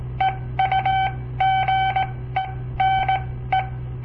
Morse code, heard as a keyed tone, about 17 wpm, spaced as for 14 wpm, 770 Hz. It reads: EUGENE
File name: papertapes / local / Code Practice-01